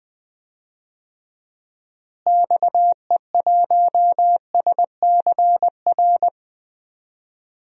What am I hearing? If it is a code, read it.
Morse code, heard as a keyed tone, about 20 wpm, 700 Hz. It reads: XE1SCR
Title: DXCC-100 Call Signs - 6 Characters - Repeated 3x - 20wpm